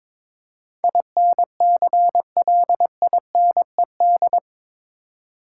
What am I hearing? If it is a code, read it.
Morse code, heard as a keyed tone, about 22 wpm, 700 Hz. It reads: INCLINED
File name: practice / 22wpm / Eight Letter Words - 22wpm